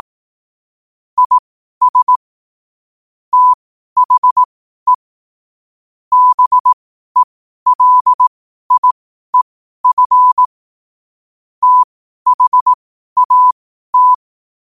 Morse code, heard as a keyed tone, about 18 wpm, spaced as for 12 wpm, 1000 Hz. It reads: IS THE BELIEF THAT
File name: qotdfiles / 12wpm